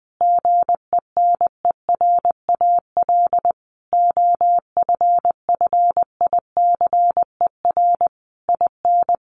Morse code, heard as a keyed tone, about 20 wpm, 700 Hz. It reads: GENERAL OFFICER IN
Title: Sentences from Top 1200 Words 20wpm